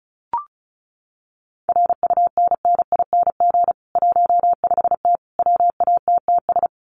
Morse code, heard as a keyed tone, about 34 wpm, 700 Hz. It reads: RUNNING 15T WATTS
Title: Common QSO Phrases 35wpm